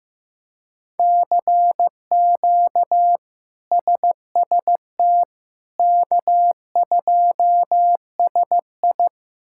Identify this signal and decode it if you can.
Morse code, heard as a keyed tone, about 15 wpm, 700 Hz. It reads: CQ SST K2SI
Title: K1USN SST QSO Practice - 15wpm